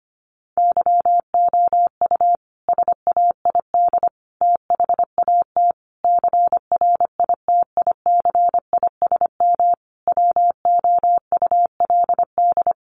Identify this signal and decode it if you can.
Morse code, heard as a keyed tone, about 25 wpm, 700 Hz. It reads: YOU SAID THAT CRITICISM WOULD
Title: Sentences from Top 1300 Words 25wpm